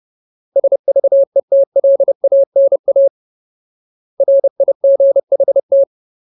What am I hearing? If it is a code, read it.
Morse code, heard as a keyed tone, about 30 wpm, 550 Hz. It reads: SVETLANA RIGHT
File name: Ultimate Word List / 30WPM Ultimate